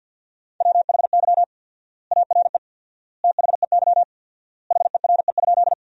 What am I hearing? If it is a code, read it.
Morse code, heard as a keyed tone, about 50 wpm, 700 Hz. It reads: WHY ARE THEY HERE?
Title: 4x Word-Spacing - Sentences from Top 200 Words - 50wpm